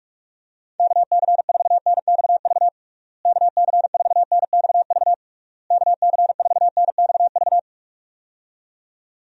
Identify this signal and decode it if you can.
Morse code, heard as a keyed tone, about 46 wpm, 700 Hz. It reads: KC4NXV KC4NXV KC4NXV
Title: USA Call Signs - 6 Characters - Repeated 3x - 45wpm